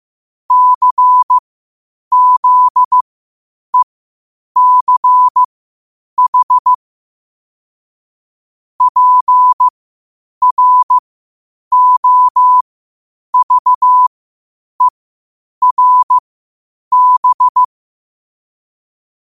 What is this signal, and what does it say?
Morse code, heard as a keyed tone, about 15 wpm, spaced as for 8 wpm, 1000 Hz. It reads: CZECH PROVERB
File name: qotdfiles / 8wpm